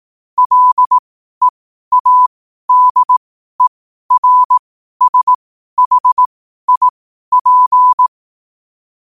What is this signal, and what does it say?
Morse code, heard as a keyed tone, about 18 wpm, spaced as for 12 wpm, 1000 Hz. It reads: LEADERSHIP
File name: qotdfiles / 12wpm